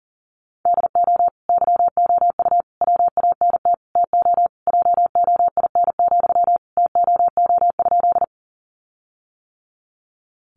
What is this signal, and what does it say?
Morse code, heard as a keyed tone, about 40 wpm, 700 Hz. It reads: DO YOU WANT TO JOIN, TOO?
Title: Sentences from Top 1400 Words 40wpm